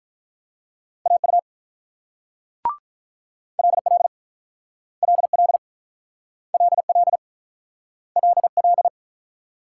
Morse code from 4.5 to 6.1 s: LL

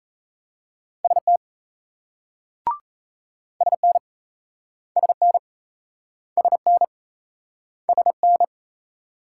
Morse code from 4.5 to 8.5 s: SN SN SN

